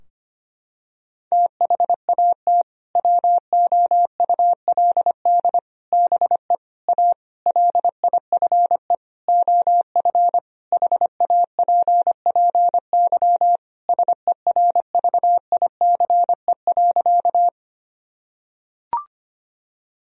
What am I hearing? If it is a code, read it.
Morse code, heard as a keyed tone, about 25 wpm, 700 Hz. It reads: THAT WOULD BE A LIFE OF HAPPY SERVICE.
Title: Sentences from Top 1000 Words - Review Part 4 - 25wpm